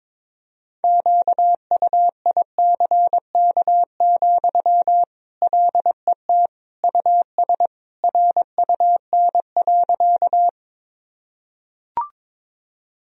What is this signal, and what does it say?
Morse code, heard as a keyed tone, about 22 wpm, 700 Hz. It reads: QUICK, LET US RUN.
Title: Sentences from Top 1000 Words - Review Part 1 - 22wpm